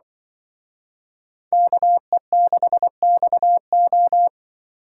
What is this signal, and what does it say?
Morse code, heard as a keyed tone, about 24 wpm, 700 Hz. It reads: KE6XO
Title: Speed Racing - USA 2x2 Call Signs 20wpm